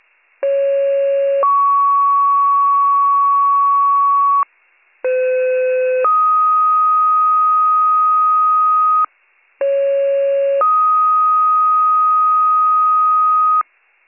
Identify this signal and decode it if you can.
Two-tone page, heard as a one-second tone, then a three-second tone, about 550 then 1120 Hz, about 510 then 1250 Hz, about 550 then 1250 Hz. It is heard from 0.4 to 13.6 s.